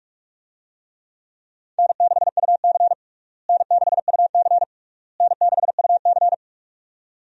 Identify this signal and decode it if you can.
Morse code, heard as a keyed tone, about 41 wpm, 700 Hz. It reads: N6UC N6UC N6UC
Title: USA Call Signs - 4 Characters - Repeated 3x - 45wpm